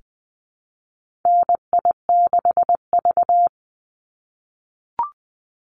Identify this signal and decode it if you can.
Morse code, heard as a keyed tone, about 20 wpm, 700 Hz. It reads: NI6V
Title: Call Signs - 20wpm